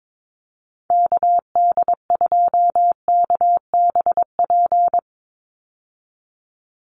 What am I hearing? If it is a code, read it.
Morse code, heard as a keyed tone, about 22 wpm, 700 Hz. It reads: KD2KBP
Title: USA Call Signs 22wpm